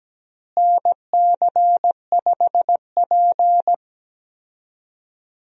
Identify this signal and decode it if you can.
Morse code, heard as a keyed tone, about 17 wpm, 700 Hz. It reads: NC5P
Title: USA Call Signs - 4 Characters - Repeated 2x - 17wpm